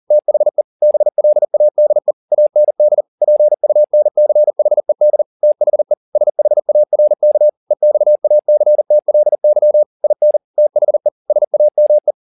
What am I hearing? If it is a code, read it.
Morse code, heard as a keyed tone, about 40 wpm, 600 Hz. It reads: THE BLADE AND PUNCHED THE SHARK EXACTLY IN THE SAME
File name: the old man and the sea 2 40wpm